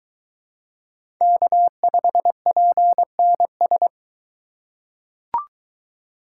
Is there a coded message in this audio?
Yes: Morse code from 1.2 to 3.9 s, heard as a keyed tone, about 22 wpm, 700 Hz. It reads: K5PNS